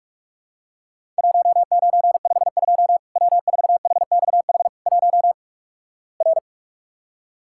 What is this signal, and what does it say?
Morse code, heard as a keyed tone, about 43 wpm, 700 Hz. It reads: 1952 W4HXH 1 R
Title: Worked All Europe DX Contest - QTC Exchange - 45wpm